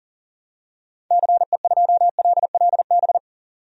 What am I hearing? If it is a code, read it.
Morse code, heard as a keyed tone, about 40 wpm, 700 Hz. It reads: CE2LLB
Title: DXCC-100 Call Signs - 6 Characters - 40wpm